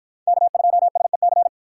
Morse code, heard as a keyed tone, about 49 wpm, 700 Hz. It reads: K2SEC